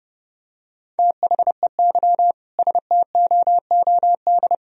Morse code, heard as a keyed tone, about 30 wpm, 700 Hz. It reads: THEY STOOD
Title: Sentences from Top 1000 Words - Review Part 3 - 30wpm